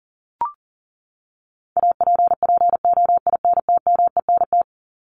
Morse code, heard as a keyed tone, about 40 wpm, 700 Hz. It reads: APPOINTMENT
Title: Top 1751-2000 Words 40wpm